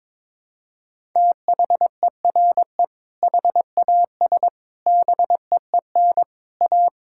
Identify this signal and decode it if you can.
Morse code, heard as a keyed tone, about 22 wpm, 700 Hz. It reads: THERE HAS BEEN A
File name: practice / 22wpm / Sentences with 4-Word Phrases - 22wpm